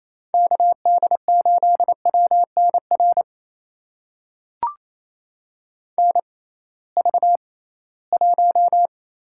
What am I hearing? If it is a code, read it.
Morse code, heard as a keyed tone, about 28 wpm, spaced as for 18 wpm, 700 Hz. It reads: KD8WNR NV1
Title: USA Call Signs - 6x Character Spacing - 28wpm